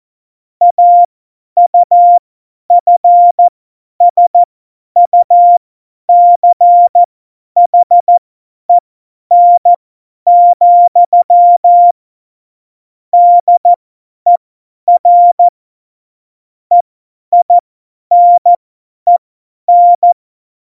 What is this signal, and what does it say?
Morse code, heard as a keyed tone, about 14 wpm, spaced as for 10 wpm, 700 Hz. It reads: AUFSUCHEN, DER EINEN
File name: Morsekurs / Witze / Witz 29. September 2020, 50 F70